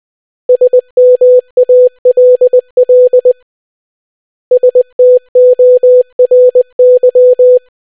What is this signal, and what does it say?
Morse code, heard as a keyed tone, about 20 wpm, 500 Hz. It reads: SMALL STORY